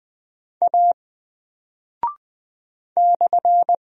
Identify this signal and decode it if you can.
Morse code, heard as a keyed tone, about 20 wpm, 700 Hz. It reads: A /
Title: Single Character (Letter:Number:Punctuation:Prosign) Alt 20wpm